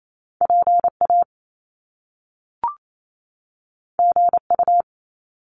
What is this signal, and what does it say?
Morse code, heard as a keyed tone, about 28 wpm, 700 Hz. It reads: PA GU
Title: Top Two Letter Combinations in Words 28wpm